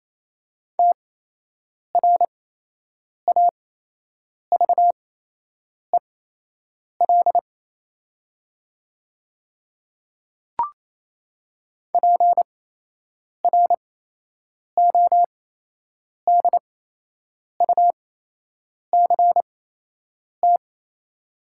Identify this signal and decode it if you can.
Morse code, heard as a keyed tone, about 28 wpm, spaced as for 8 wpm, 700 Hz. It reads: TRAVEL PRODUCT
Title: 8x Character Spacing - Top 500 Words - 28wpm